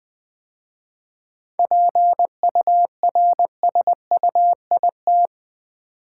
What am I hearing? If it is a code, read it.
Morse code, heard as a keyed tone, about 21 wpm, 700 Hz. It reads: PURSUIT
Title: 7 Letter Words - First 3 Letters Given - 20wpm